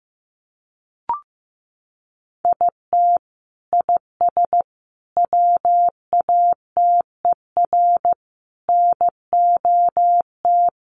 Morse code, heard as a keyed tone, about 15 wpm, 700 Hz. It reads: IT IS WATER NOT
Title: Sentences from Top 1100 Words 15wpm